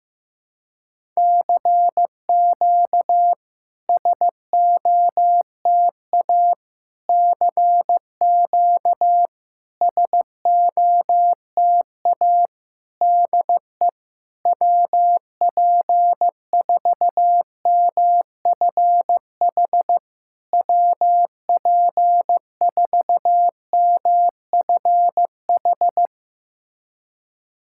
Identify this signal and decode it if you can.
Morse code, heard as a keyed tone, about 16 wpm, 700 Hz. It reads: CQ SOTA CQ SOTA DE WP4MFH WP4MFH
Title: SOTA QSO Practice 15wpm